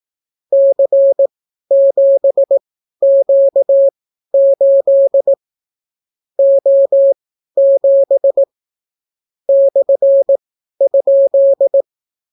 Morse code, heard as a keyed tone, about 18 wpm, spaced as for 12 wpm, 550 Hz. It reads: C7Q8 O7 /?